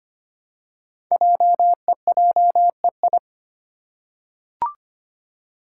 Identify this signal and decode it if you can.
Morse code, heard as a keyed tone, about 25 wpm, 700 Hz. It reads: JEJEI
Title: Code Groups - 5 Letter:Numbers - 25wpm